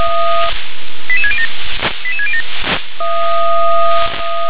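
Five-tone selective call (ZVEI), heard as quick runs of five tones, tones of about 70 ms. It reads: 88607 88607
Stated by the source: ZVEI five-tone selective call 88753 received via SDR.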